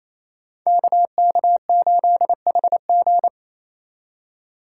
Morse code, heard as a keyed tone, about 27 wpm, 700 Hz. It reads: KK8HG
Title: Code Groups - 5 Letter:Numbers - 28wpm